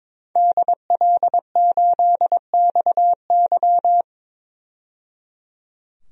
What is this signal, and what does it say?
Morse code, heard as a keyed tone, about 22 wpm, 700 Hz. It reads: DL8XY
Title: DXCC-100 Call Signs - 5 Characters - 22wpm